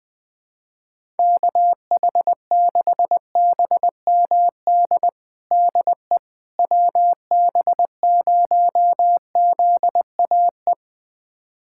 Morse code, heard as a keyed tone, about 20 wpm, 700 Hz. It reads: KH6BMD DE WB0ZAE
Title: USA Callsign DE Callsign - 6 Characters - 20wpm